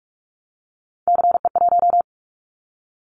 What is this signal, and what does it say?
Morse code, heard as a keyed tone, about 44 wpm, 700 Hz. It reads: CE1
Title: Callsign Prefixes for Top 100 DXCC - 45wpm